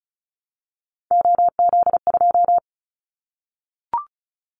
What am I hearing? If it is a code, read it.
Morse code, heard as a keyed tone, about 34 wpm, 700 Hz. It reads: OZ2
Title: Callsign Prefixes for Top 100 DXCC - 35wpm